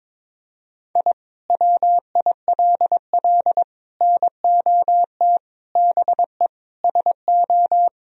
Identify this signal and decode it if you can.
Morse code, heard as a keyed tone, about 22 wpm, 700 Hz. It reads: I WILL NOT BE SO
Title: Sentences from Top 1000 Words - Review Part 1 - 22wpm